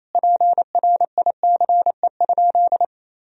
Morse code, heard as a keyed tone, about 28 wpm, 700 Hz. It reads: PRICE?
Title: Sentences from Top 1000 Words - Review Part 3 - 28wpm